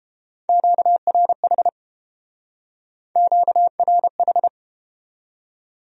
Morse code, heard as a keyed tone, about 31 wpm, 700 Hz. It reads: QRH QRH